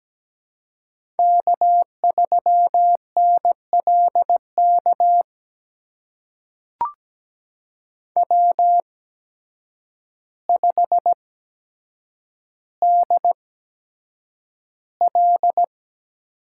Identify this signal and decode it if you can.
Morse code, heard as a keyed tone, about 17 wpm, spaced as for 8 wpm, 700 Hz. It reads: K3NLK W5DL
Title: USA Call Signs - 5 Characters - 8x Character Spacing - 17wpm